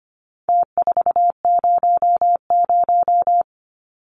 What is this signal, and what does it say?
Morse code, heard as a keyed tone, about 25 wpm, 700 Hz. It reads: T400
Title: Numbers - Contest Style - 4-Digit 25wpm